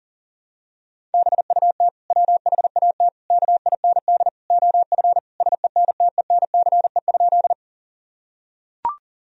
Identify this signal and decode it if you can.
Morse code, heard as a keyed tone, about 40 wpm, 700 Hz. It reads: BUT WHAT KIND OF SENTENCE?
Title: Sentences from Top 1000 Words - Review Part 1 - 40wpm